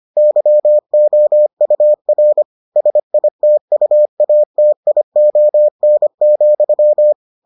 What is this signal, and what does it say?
Morse code, heard as a keyed tone, about 25 wpm, 600 Hz. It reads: YOUR SITUATION,